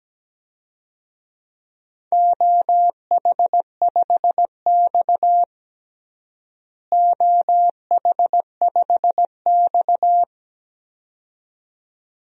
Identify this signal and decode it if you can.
Morse code, heard as a keyed tone, about 17 wpm, 700 Hz. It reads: OH5X OH5X
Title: DXCC-100 Call Signs - 4 Characters - Repeated 2x - 17wpm